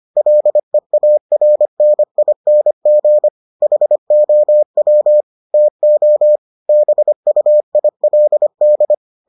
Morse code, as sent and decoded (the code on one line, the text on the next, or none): .-.. .- .-. -. .. -. --. / .... --- .-- / - --- / -... ..- .. .-.. -..
LEARNING HOW TO BUILD